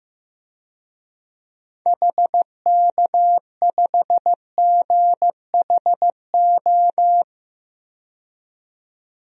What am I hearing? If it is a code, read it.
Morse code, heard as a keyed tone, about 15 wpm, 700 Hz. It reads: HK5GHO
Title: DXCC-100 Call Signs - 6 Characters - 15wpm